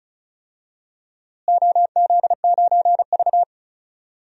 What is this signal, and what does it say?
Morse code, heard as a keyed tone, about 36 wpm, 700 Hz. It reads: OZ9V
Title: DXCC-100 Call Signs - 4 Characters - Repeated 2x - 35wpm